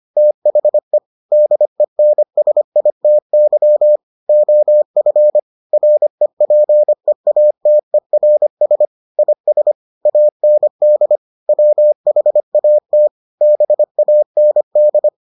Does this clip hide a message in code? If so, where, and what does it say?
Yes: Morse code from 0.2 to 15.1 s, heard as a keyed tone, about 25 wpm, 600 Hz. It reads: THE DENSITY OF REPEATERS IS AND WHAT BAND